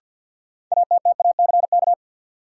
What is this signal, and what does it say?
Morse code, heard as a keyed tone, about 50 wpm, 700 Hz. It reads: ATTACK
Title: Sentences from Top 1500 Words - New Word Drill Alt - 50wpm